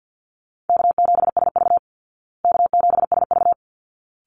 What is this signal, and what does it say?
Morse code, heard as a keyed tone, about 50 wpm, 700 Hz. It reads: K7SV K7SV
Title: Easy Call Signs Repeated 2x - 50wpm